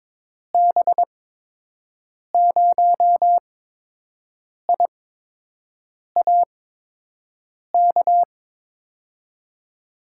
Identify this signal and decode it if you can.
Morse code, heard as a keyed tone, about 22 wpm, spaced as for 6 wpm, 700 Hz. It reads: B0IAK
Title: USA Call Signs - 6 Characters - 8x Character Spacing - 22wpm